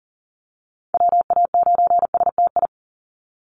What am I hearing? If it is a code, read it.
Morse code, heard as a keyed tone, about 40 wpm, 700 Hz. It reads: WA9STI
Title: NA SOTA Activator Callsigns 40wpm